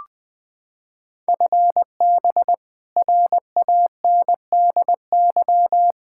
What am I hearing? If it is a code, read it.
Morse code, heard as a keyed tone, about 20 wpm, 700 Hz. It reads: FB RANDY
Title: K1USN SST QSO Practice 20wpm